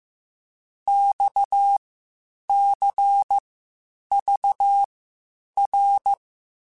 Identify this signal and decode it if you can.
Morse code, heard as a keyed tone, about 15 wpm, spaced as for 8 wpm, 800 Hz. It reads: XCVR
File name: (spaced) / QSOs Spaced-mp3s / WD0FBT 1 @13 sp